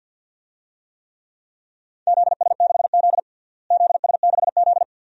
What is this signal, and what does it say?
Morse code, heard as a keyed tone, about 52 wpm, 700 Hz. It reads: ZS6Z ZS6Z
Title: DXCC-100 Call Signs - 4 Characters - Repeated 2x - 50wpm